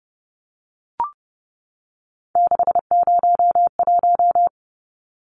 Morse code, heard as a keyed tone, about 30 wpm, 700 Hz. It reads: 601
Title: Numbers - 3-Digit 30wpm